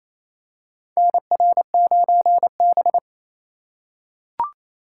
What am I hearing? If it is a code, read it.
Morse code, heard as a keyed tone, about 28 wpm, 700 Hz. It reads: NR9B
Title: USA Call Signs - 4 Characters - 2x Character Spacing - 28wpm